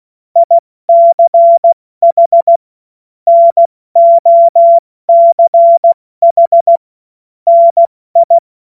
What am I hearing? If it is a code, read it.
Morse code, heard as a keyed tone, about 16 wpm, spaced as for 14 wpm, 680 Hz. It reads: ICH NOCH NI
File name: Witz-05-August-2025-70-F80